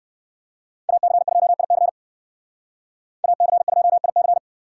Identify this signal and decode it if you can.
Morse code, heard as a keyed tone, about 59 wpm, 700 Hz. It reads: AC2IC AC2IC